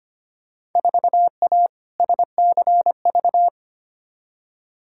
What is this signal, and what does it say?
Morse code, heard as a keyed tone, about 25 wpm, 700 Hz. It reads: 4A SCV
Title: Field Day Exchanges - 25wpm